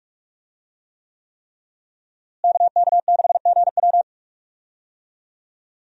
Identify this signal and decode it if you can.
Morse code, heard as a keyed tone, about 42 wpm, 700 Hz. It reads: KK6GW
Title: USA Call Signs - 5 Characters - Repeated 3x - 45wpm